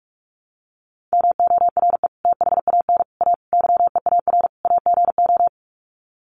Morse code, heard as a keyed tone, about 45 wpm, 700 Hz. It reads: MORE THAN A YEAR AGO
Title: Sets of 5 Words Encore2 45wpm